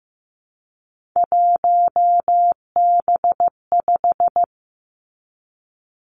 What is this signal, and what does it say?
Morse code, heard as a keyed tone, about 15 wpm, 700 Hz. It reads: 1B5
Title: Three Challenging Letter-Number Combinations 15wpm